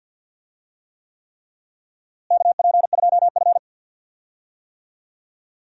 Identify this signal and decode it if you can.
Morse code, heard as a keyed tone, about 52 wpm, 700 Hz. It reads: KP2F
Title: DXCC-100 Call Signs - 4 Characters - Repeated 3x - 50wpm